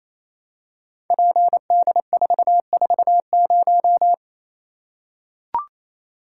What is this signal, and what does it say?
Morse code, heard as a keyed tone, about 28 wpm, 700 Hz. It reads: PD440
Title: Code Groups - 5 Letter:Numbers - 28wpm